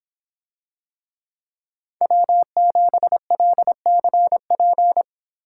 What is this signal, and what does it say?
Morse code, heard as a keyed tone, about 26 wpm, 700 Hz. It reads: W7LCP